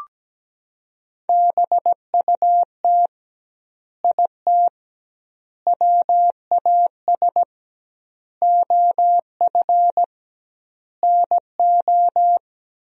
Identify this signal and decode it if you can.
Morse code, heard as a keyed tone, about 17 wpm, 700 Hz. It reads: BUT IT WAS OF NO
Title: 2x Word-Spacing - Sentences from Top 100 Words - 17wpm